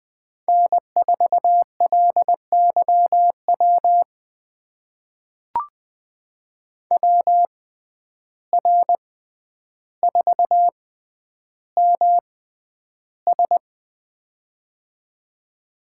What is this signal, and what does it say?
Morse code, heard as a keyed tone, about 20 wpm, spaced as for 10 wpm, 700 Hz. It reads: N4LYW WR4MS
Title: USA Call Signs - 6x Character Spacing - 20wpm